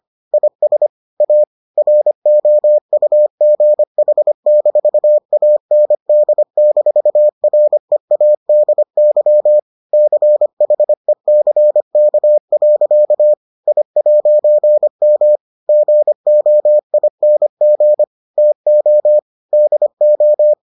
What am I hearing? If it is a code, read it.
Morse code, heard as a keyed tone, about 25 wpm, 600 Hz. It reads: IS A ROUGH-AND-READY CHECK. I'M GOING TO DO